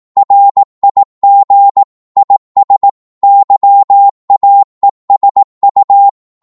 Morse code, heard as a keyed tone, about 18 wpm, 800 Hz. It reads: RIG IS YAESU
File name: Morse / qso-18wpm